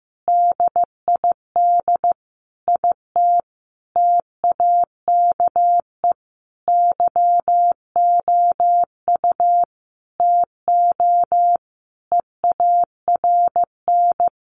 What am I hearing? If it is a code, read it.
Morse code, heard as a keyed tone, about 14 wpm, 700 Hz. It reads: DID IT TAKE YOU TO EARN
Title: Sentences from Top 1500 Words 15wpm